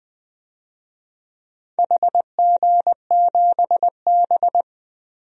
Morse code, heard as a keyed tone, about 19 wpm, 700 Hz. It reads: HG7B
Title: DXCC-100 Call Signs - 4 Characters - Repeated 2x - 20wpm